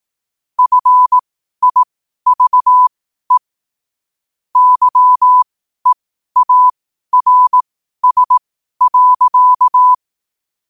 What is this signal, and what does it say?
Morse code, heard as a keyed tone, about 18 wpm, spaced as for 12 wpm, 1000 Hz. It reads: FIVE YEARS.
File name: qotdfiles / 12wpm